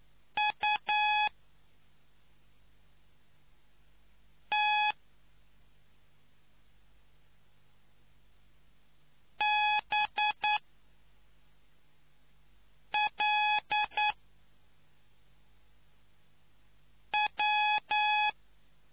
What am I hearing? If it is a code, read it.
Morse code, heard as a keyed tone, about 9 wpm, 840 Hz. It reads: U T B L W